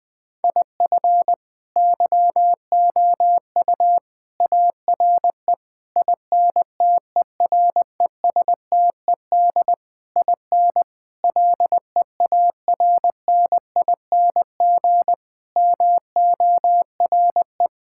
Morse code, heard as a keyed tone, about 20 wpm, 700 Hz. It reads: IF YOU ARE INTERESTED IN LEARNING MORE
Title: Sentences with 8-Word Phrases - 20wpm